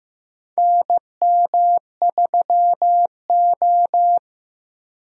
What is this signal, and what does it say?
Morse code, heard as a keyed tone, about 15 wpm, 700 Hz. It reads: NM3O
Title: USA Call Signs - 4 Characters - Repeated 3x - 15wpm